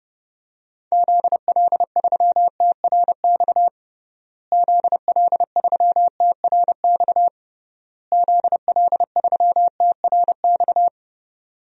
Morse code, heard as a keyed tone, about 32 wpm, 700 Hz. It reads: ZL3TRX ZL3TRX ZL3TRX
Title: DXCC-100 Call Signs - 6 Characters - Repeated 3x - 30wpm